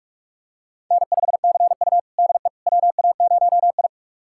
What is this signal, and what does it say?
Morse code, heard as a keyed tone, about 46 wpm, 690 Hz. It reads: N5CU DE WA0I
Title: USA Callsign DE Callsign - 4 Characters - 45wpm